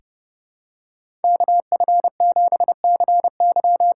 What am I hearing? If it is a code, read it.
Morse code, heard as a keyed tone, about 30 wpm, 700 Hz. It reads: KF7CY
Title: USA Call Signs - 6 Characters - 8x Character Spacing - 30wpm